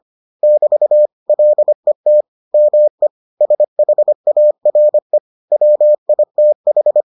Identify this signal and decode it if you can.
Morse code, heard as a keyed tone, about 25 wpm, 600 Hz. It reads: <BT> LET ME SHARE WITH